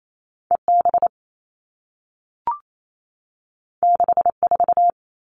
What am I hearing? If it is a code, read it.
Morse code, heard as a keyed tone, about 28 wpm, 700 Hz. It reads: EB 64
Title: All Cut Numbers - 2-Digit 28wpm